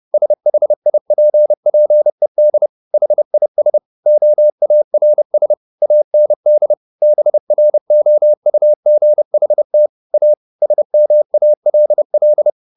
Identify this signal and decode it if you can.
Morse code, heard as a keyed tone, about 30 wpm, 600 Hz. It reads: SHIPPED HIS OARS AND BROUGHT A SMALL
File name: the old man and the sea / the old man and the sea 1 30wpm